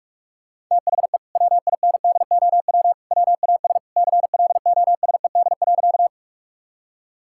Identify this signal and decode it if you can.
Morse code, heard as a keyed tone, about 46 wpm, 700 Hz. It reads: THE WINDOW WAS CLOSED.
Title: Sentences from Top 1600 Words 45wpm